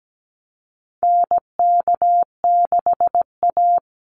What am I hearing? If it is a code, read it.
Morse code, heard as a keyed tone, about 17 wpm, 700 Hz. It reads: NK6A
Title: NA SOTA Chaser Callsigns 17wpm